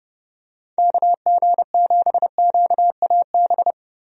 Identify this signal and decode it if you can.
Morse code, heard as a keyed tone, about 30 wpm, 700 Hz. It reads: KG7QAB
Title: USA Call Signs - 6 Characters - Repeated 3x - 30wpm